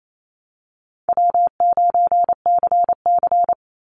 Morse code, heard as a keyed tone, about 28 wpm, 700 Hz. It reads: W9CC
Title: Easy Call Signs Repeated 2x - 28wpm